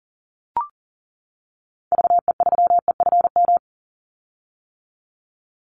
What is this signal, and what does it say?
Morse code, heard as a keyed tone, about 40 wpm, 700 Hz. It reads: VE3EFM
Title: Call Signs - 40wpm